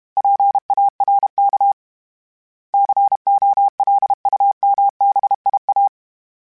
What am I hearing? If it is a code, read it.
Morse code, heard as a keyed tone, about 32 wpm, 800 Hz. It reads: PARK COLUMBIA
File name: fromSMs / FromSMs mp3s / Cities@30